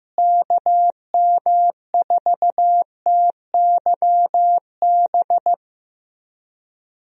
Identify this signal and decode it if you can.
Morse code, heard as a keyed tone, about 15 wpm, 700 Hz. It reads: KM4TYB
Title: USA Call Signs - 6 Characters - Repeated 2x - 15wpm